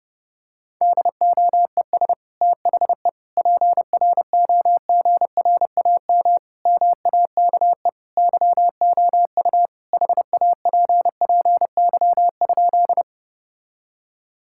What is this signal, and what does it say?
Morse code, heard as a keyed tone, about 32 wpm, 700 Hz. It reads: DOES THE PROGRAM MAKE YOU HAPPY?
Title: Sentences from Top 1000 Words - Review Part 5 - 30wpm